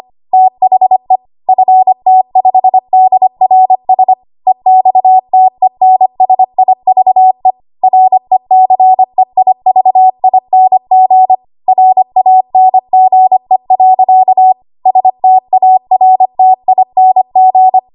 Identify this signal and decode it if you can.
Morse code, heard as a keyed tone, about 25 wpm, 750 Hz. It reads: THE FT5DRS EXTENSIVE RECEIVING RANGE. STARTING
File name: Archive / 25 WPM / 25WPM